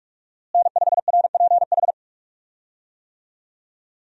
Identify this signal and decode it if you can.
Morse code, heard as a keyed tone, about 44 wpm, 700 Hz. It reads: N5RPH
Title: USA Call Signs - 5 Characters - Repeated 2x - 45wpm